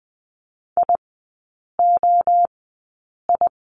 Morse code, heard as a keyed tone, about 20 wpm, spaced as for 8 wpm, 700 Hz. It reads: IOI
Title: 067 Lesson 12e - Letter d - Three Characters 20wpm